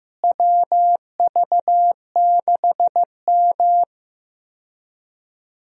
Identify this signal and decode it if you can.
Morse code, heard as a keyed tone, about 15 wpm, 700 Hz. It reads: WV6M
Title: USA Call Signs - 4 Characters - Repeated 3x - 15wpm